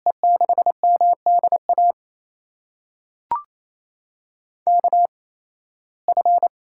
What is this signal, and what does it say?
Morse code, heard as a keyed tone, about 28 wpm, spaced as for 18 wpm, 700 Hz. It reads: E6MDA KF
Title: USA Call Signs - 6 Characters - 8x Character Spacing - 28wpm